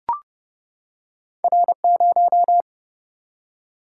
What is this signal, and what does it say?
Morse code, heard as a keyed tone, about 31 wpm, 700 Hz. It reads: R0
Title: Single-Letter Single-Number 30wpm